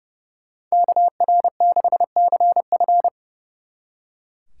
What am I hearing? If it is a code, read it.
Morse code, heard as a keyed tone, about 36 wpm, 700 Hz. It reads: KR6CF
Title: Speed Racing - USA Call Signs 30wpm